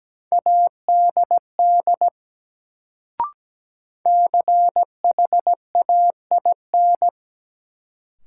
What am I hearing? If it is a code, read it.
Morse code, heard as a keyed tone, about 17 wpm, 700 Hz. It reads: ADD CHAIN
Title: Sentences from Top 1500 Words - New Word Drill Alt - 17wpm